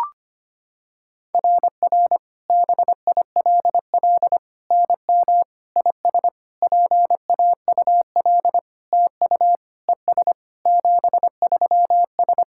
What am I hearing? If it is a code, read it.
Morse code, heard as a keyed tone, about 25 wpm, 700 Hz. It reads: RR BILL NM IS PAUL TU ES 73S
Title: SOTA QSO Practice 25wpm